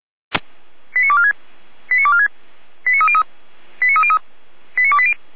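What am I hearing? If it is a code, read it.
Five-tone selective call (ZVEI), heard as quick runs of five tones, tones of about 70 ms. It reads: 89246 89246 89303 89303 89280
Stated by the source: ZVEI five-tone call 89121 received on SDR.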